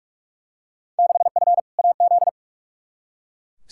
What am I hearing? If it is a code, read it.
Morse code, heard as a keyed tone, about 41 wpm, 700 Hz. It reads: 6F AZ
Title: Field Day Exchanges Encore - 45wpm